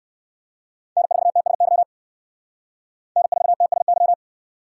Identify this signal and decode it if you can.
Morse code, heard as a keyed tone, about 55 wpm, 700 Hz. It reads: N4TSY N4TSY
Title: Speed Racing - USA 1x3 Call Signs 45wpm